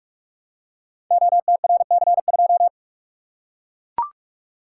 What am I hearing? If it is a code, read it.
Morse code, heard as a keyed tone, about 44 wpm, 700 Hz. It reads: OTRC2
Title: Code Groups - 5 Letter:Numbers - 45wpm